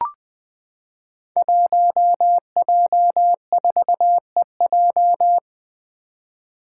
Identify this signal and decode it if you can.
Morse code, heard as a keyed tone, about 20 wpm, 700 Hz. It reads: 1J4EJ
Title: Five Challenging Letter-Number Combinations 20wpm